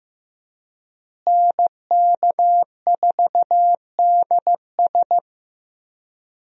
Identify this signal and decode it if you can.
Morse code, heard as a keyed tone, about 15 wpm, 700 Hz. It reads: NK4DS
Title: USA Call Signs - 5 Characters - Repeated 2x - 15wpm